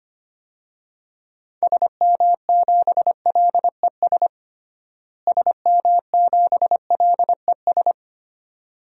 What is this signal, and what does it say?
Morse code, heard as a keyed tone, about 25 wpm, 700 Hz. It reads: SM7LES SM7LES
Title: DXCC-100 Call Signs - 6 Characters - Repeated 2x - 25wpm